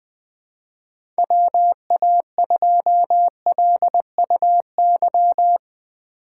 Morse code, heard as a keyed tone, about 20 wpm, 700 Hz. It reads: WA2LUY
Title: USA Call Signs - 6 Characters - 3x Character Spacing - 20wpm